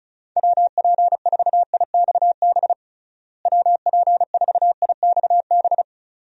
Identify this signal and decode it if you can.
Morse code, heard as a keyed tone, about 34 wpm, 700 Hz. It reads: WP4IXB WP4IXB
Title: USA Call Signs - 6 Characters - Repeated 2x - 35wpm